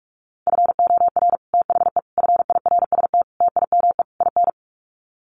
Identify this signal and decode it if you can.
Morse code, heard as a keyed tone, about 44 wpm, 700 Hz. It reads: FOR THE FIRST TIME IN
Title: Sets of 5 Words 45wpm